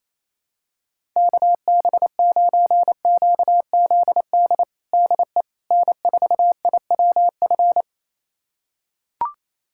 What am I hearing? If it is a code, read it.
Morse code, heard as a keyed tone, about 28 wpm, 700 Hz. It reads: KB9QZD DE N4IWF
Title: USA Callsign DE Callsign 28wpm